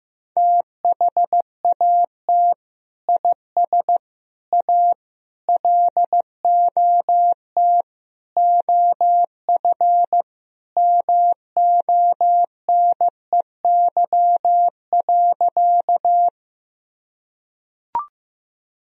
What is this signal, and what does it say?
Morse code, heard as a keyed tone, about 15 wpm, 700 Hz. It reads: THAT IS A LOT OF MONEY.
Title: Sentences from Top 1000 Words - Review Part 3 - 15wpm